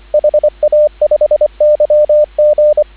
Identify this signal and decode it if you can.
Morse code, heard as a keyed tone, about 24 wpm, 590 Hz, with radio noise. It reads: HA5YG